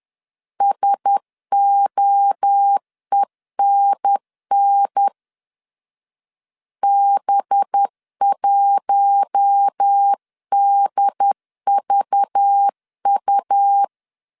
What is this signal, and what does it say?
Morse code, heard as a keyed tone, about 10 wpm, 800 Hz. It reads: SOENN B1DVU
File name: CW / DET / 1 6 1500C S B 10wpm 4137